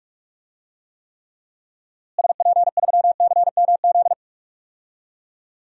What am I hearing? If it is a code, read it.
Morse code, heard as a keyed tone, about 41 wpm, 700 Hz. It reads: SP3CMZ
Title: DXCC-100 Call Signs - 6 Characters - Repeated 2x - 45wpm